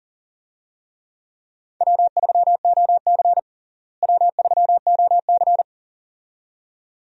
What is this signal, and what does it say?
Morse code, heard as a keyed tone, about 40 wpm, 700 Hz. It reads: W3OC W3OC
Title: USA Call Signs - 4 Characters - Repeated 2x - 40wpm